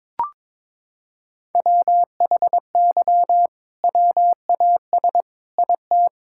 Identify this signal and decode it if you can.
Morse code, heard as a keyed tone, about 22 wpm, 700 Hz. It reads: WHY WAS IT
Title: Sentences from Top 1000 Words - Review Part 5 - 22wpm